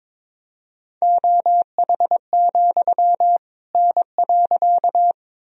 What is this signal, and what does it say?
Morse code, heard as a keyed tone, about 22 wpm, 700 Hz. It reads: OH, N.
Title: Sentences from Top 500 Words and Abbreviations 22wpm